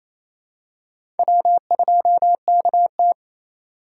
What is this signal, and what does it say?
Morse code, heard as a keyed tone, about 27 wpm, 700 Hz. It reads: W2KT
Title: USA Call Signs - 4 Characters - 2x Character Spacing - 28wpm